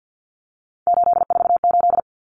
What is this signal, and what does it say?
Morse code, heard as a keyed tone, about 48 wpm, 700 Hz. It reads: 848
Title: Numbers - 3-Digit 50wpm